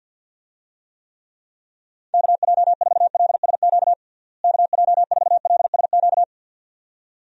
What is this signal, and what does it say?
Morse code, heard as a keyed tone, about 53 wpm, 700 Hz. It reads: KJ4LSQ KJ4LSQ
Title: USA Call Signs - 6 Characters - Repeated 2x - 50wpm